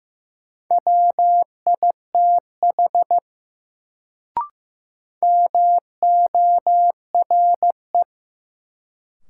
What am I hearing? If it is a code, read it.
Morse code, heard as a keyed tone, about 15 wpm, 700 Hz. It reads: WITH MORE
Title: Sentences from Top 100 Words - New Word Drill - 15wpm